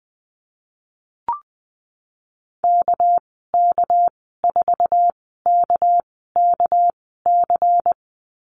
Morse code, heard as a keyed tone, about 20 wpm, spaced as for 15 wpm, 700 Hz. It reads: KK4KKC
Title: 207 Lesson 33g - Letter j - Call Signs 20wpm